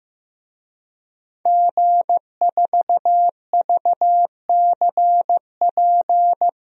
Morse code, heard as a keyed tone, about 16 wpm, 700 Hz. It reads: G4VCP
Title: DXCC-100 Call Signs - 5 Characters - Repeated 2x - 15wpm